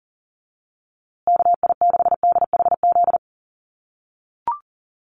Morse code, heard as a keyed tone, about 40 wpm, 700 Hz. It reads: KI6DHZ
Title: USA Call Signs Repeated 2x 40wpm